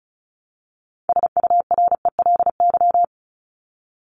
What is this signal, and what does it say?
Morse code, heard as a keyed tone, about 34 wpm, 700 Hz. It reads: SURELY
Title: LY Suffix - Top 100 Words with Base Words 35wpm